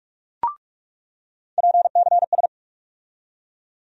Morse code, heard as a keyed tone, about 46 wpm, 700 Hz. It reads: PCS
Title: USA Call Sign Postfixes - 3 Letters 45wpm